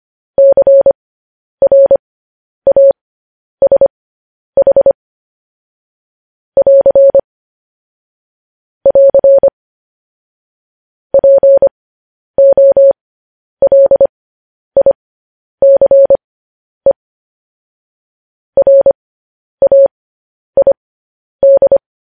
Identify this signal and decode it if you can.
Morse code, heard as a keyed tone, about 25 wpm, spaced as for 10 wpm, 550 Hz. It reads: CRASH <AR> <AR> POLICE RAID
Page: news headlines in morse code at 10 wpm